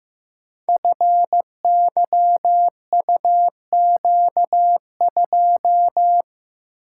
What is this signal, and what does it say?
Morse code, heard as a keyed tone, about 16 wpm, 700 Hz. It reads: FYUQ2
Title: Code Groups - 5 Letter:Numbers - 15wpm